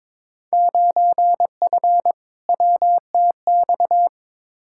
Morse code, heard as a keyed tone, about 22 wpm, 700 Hz. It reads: 9F WTX
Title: Field Day Exchanges Encore - 22wpm